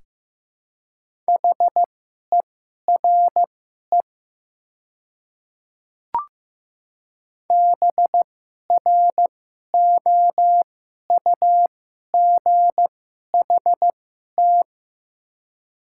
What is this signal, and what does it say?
Morse code, heard as a keyed tone, about 15 wpm, spaced as for 11 wpm, 700 Hz. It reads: HERE BROUGHT
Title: 2x Character Spacing - Top 500 Words - 15wpm